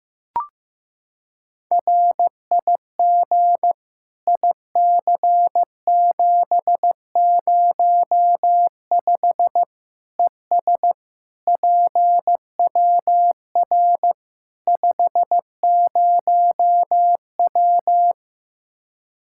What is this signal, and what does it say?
Morse code, heard as a keyed tone, about 15 wpm, 700 Hz. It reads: RIG IC705 ES PWR 50W
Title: Practice QSO - Exchange 2 - Info - 15wpm